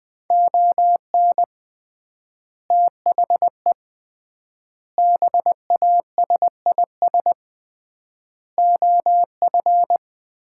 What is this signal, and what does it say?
Morse code, heard as a keyed tone, about 21 wpm, 700 Hz. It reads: ON THE BASIS OF
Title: 3x Word-Spacing - Sets of 5 Words - 20wpm